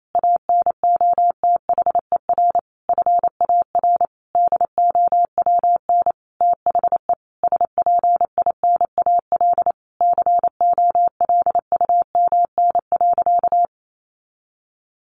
Morse code, heard as a keyed tone, about 28 wpm, 700 Hz. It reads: ANOTHER FAR DOWN THE SPINAL COLUMN.
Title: A Sound Of Thunder 28wpm